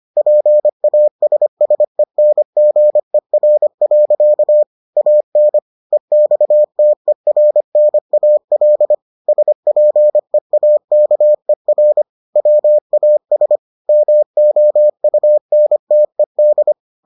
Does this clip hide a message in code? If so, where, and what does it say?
Yes: Morse code from 0.2 to 16.7 s, heard as a keyed tone, about 25 wpm, 600 Hz. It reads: PASSENGER. AN EXTERNAL SPEAKER WAS MOUNTED